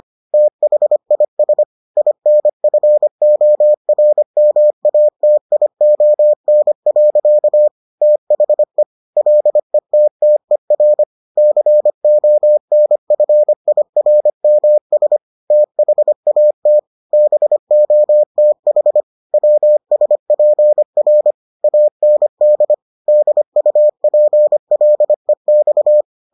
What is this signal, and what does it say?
Morse code, heard as a keyed tone, about 25 wpm, 600 Hz. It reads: THIS INFORMATION. THE LETTER CONFIRMS THAT BOTH WSPR AND DUPLEX